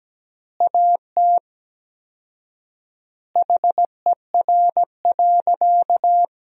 Morse code, heard as a keyed tone, about 17 wpm, 700 Hz. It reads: AT HER.
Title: 4x Word-Spacing - Sentences from Top 200 Words - 17wpm